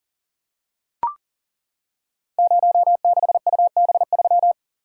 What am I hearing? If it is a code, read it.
Morse code, heard as a keyed tone, about 39 wpm, 690 Hz. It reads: 06UB3
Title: Code Groups - 5 Letter:Numbers - 40wpm